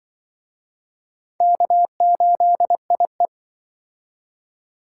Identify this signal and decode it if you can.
Morse code, heard as a keyed tone, about 25 wpm, 700 Hz. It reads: K8IE
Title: Speed Racing - USA 1x2 Call Signs 20wpm